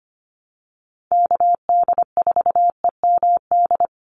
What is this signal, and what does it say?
Morse code, heard as a keyed tone, about 25 wpm, 700 Hz. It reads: KD4EMD
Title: USA Call Signs 25wpm